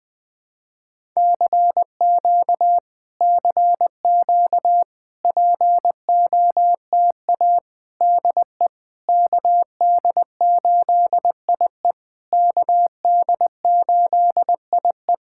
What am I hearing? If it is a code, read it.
Morse code, heard as a keyed tone, about 20 wpm, 700 Hz. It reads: CQ CQ POTA DE KD8IE KD8IE
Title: POTA QSO Practice - 20wpm